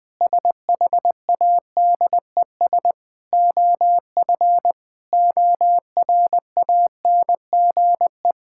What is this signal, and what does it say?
Morse code, heard as a keyed tone, about 20 wpm, 700 Hz. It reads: SHADES OF ORANGE
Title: Difficult Sentences with Hints - 20wpm